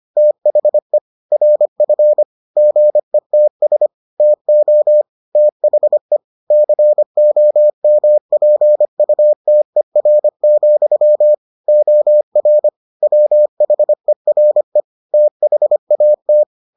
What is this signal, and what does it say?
Morse code, heard as a keyed tone, about 25 wpm, 600 Hz. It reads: THE RF GETS TO THE COMPUTER, OR WHERE THAT